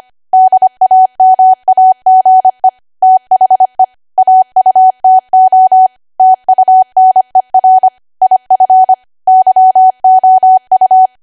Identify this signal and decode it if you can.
Morse code, heard as a keyed tone, about 25 wpm, 750 Hz. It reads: DAMAGE THE AUTO TUNER IF YOU
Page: 25 WPM Code Archive